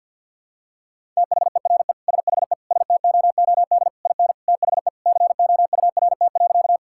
Morse code, heard as a keyed tone, about 50 wpm, 700 Hz. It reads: THERE SHE STOOD IN THE COURT.
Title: Sentences from Top 1600 Words 50wpm